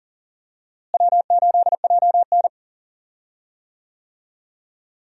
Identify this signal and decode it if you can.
Morse code, heard as a keyed tone, about 38 wpm, 700 Hz. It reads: W8JN
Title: USA Call Signs - 4 Characters - Repeated 2x - 40wpm